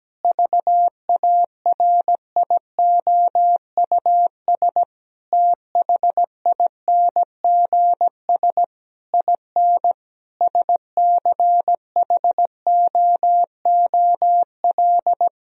Morse code, heard as a keyed tone, about 17 wpm, 700 Hz. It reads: VARIOUS THINGS IN SCHOOL
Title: Sentences from Top 1000 Words - Review Part 4 - 17wpm